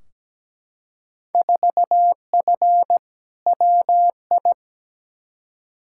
Morse code, heard as a keyed tone, about 17 wpm, 700 Hz. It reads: 4F WI
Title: Field Day Exchanges Encore - 17wpm